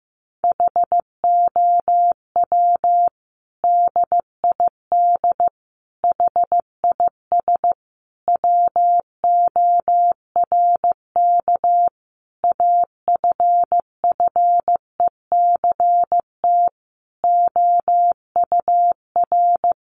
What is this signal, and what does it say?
Morse code, heard as a keyed tone, about 15 wpm, 700 Hz. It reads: HOW DID HIS WORK AFFECT OUR
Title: Sentences from Top 900 Words 15wpm